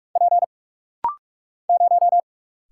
Morse code, heard as a keyed tone, about 45 wpm, 700 Hz. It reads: P 0
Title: Single Letter-Number Alt - Rapid-Fire 45wpm